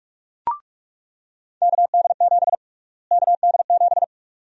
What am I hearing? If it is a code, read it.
Morse code, heard as a keyed tone, about 47 wpm, 700 Hz. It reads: KD7 KD7
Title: USA Call Sign Prefixes - Repeated 2x - 45wpm